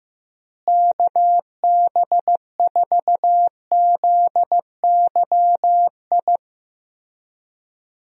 Morse code, heard as a keyed tone, about 15 wpm, 700 Hz. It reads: KB4ZYI
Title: USA Call Signs - 6 Characters - Repeated 2x - 15wpm